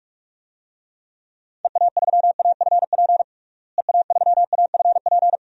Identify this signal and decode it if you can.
Morse code, heard as a keyed tone, about 46 wpm, 700 Hz. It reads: EA3AFP EA3AFP
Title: DXCC-100 Call Signs - 6 Characters - Repeated 2x - 45wpm